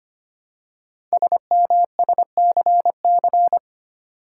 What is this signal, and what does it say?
Morse code, heard as a keyed tone, about 25 wpm, 700 Hz. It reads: SMSCC
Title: Code Groups - 5 Letters - 25wpm